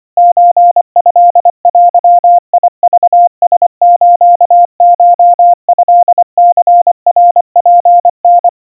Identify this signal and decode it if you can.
Morse code (Japanese Wabun code), heard as a keyed tone, about 25 wpm, 700 Hz. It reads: ソトデクラスコトニナツタ
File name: 90 cpm 700 Hz Crispy